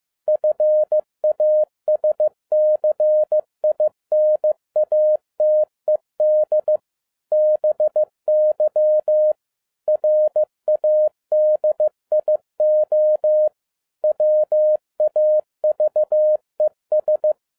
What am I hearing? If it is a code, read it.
Morse code, heard as a keyed tone, about 15 wpm, 610 Hz. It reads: FASCINATED BY RADIO WAVES